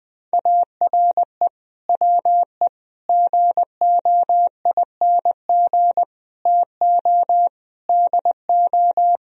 Morse code, heard as a keyed tone, about 20 wpm, 700 Hz. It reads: ARE WE GOING TO DO
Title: Speed-Racing - Sets of 5 Words 20wpm